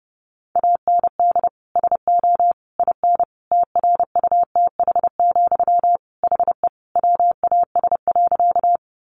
Morse code, heard as a keyed tone, about 30 wpm, 700 Hz. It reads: AND SO IN TRUTH, HE WAS.
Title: Sentences from Top 800 Words 30wpm